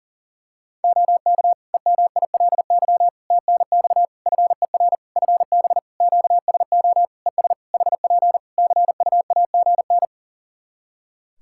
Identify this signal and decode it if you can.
Morse code, heard as a keyed tone, about 40 wpm, 700 Hz. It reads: OK EMILY TNX FER FB QSO ES HP CUAGN
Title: Practice QSO - Exchange 4 - Ending - 40wpm